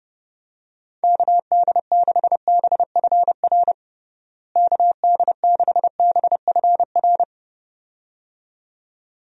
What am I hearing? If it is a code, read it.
Morse code, heard as a keyed tone, about 30 wpm, 700 Hz. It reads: KD6BFR KD6BFR
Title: USA Call Signs - 6 Characters - Repeated 2x - 30wpm